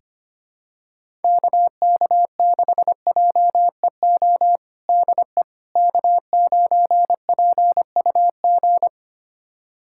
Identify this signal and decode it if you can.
Morse code, heard as a keyed tone, about 25 wpm, 700 Hz. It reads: KK6JEO DE K9PUG
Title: USA Callsign DE Callsign 25wpm